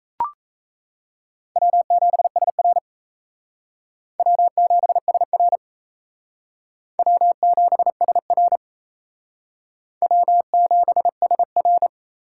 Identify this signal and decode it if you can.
Morse code, heard as a keyed tone, about 34 wpm, 700 Hz. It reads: W7SR W7SR W7SR W7SR
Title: Speed Racing - USA 1x2 Call Signs 28wpm